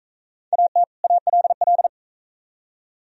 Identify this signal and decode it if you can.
Morse code, heard as a keyed tone, about 40 wpm, 700 Hz. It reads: AT ALL